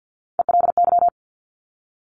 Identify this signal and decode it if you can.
Morse code, heard as a keyed tone, about 50 wpm, 700 Hz. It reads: ELY
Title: Top Three Letter Combinations in Words 50wpm